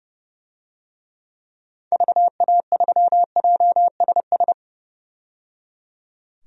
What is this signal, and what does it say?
Morse code, heard as a keyed tone, about 31 wpm, 700 Hz. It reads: VA3JSS
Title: DXCC-100 Call Signs - 6 Characters - 30wpm